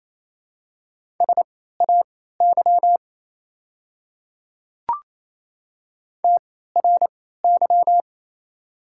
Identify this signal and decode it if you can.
Morse code, heard as a keyed tone, about 28 wpm, spaced as for 16 wpm, 700 Hz. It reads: SAY TRY
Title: 3x Character Spacing - Top 200 Words - 28wpm